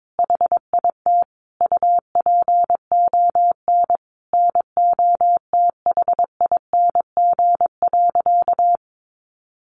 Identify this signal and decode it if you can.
Morse code, heard as a keyed tone, about 22 wpm, 700 Hz. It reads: HIT UPON NOTHING.
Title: Sentences from Top 1200 Words 22wpm